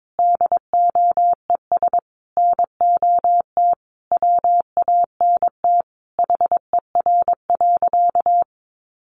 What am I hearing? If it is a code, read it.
Morse code, heard as a keyed tone, about 22 wpm, 700 Hz. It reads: DOES NOT WANT HER.
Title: Sentences from Top 800 Words 22wpm